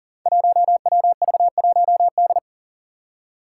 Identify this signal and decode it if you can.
Morse code, heard as a keyed tone, about 41 wpm, 700 Hz. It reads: 1WV1D
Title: Five Challenging Letter-Number Combinations 40wpm